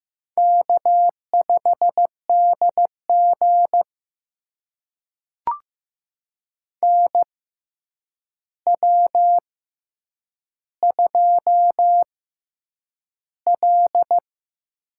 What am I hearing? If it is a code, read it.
Morse code, heard as a keyed tone, about 15 wpm, spaced as for 8 wpm, 700 Hz. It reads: K5DG NW2L
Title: USA Call Signs - 6x Character Spacing - 15wpm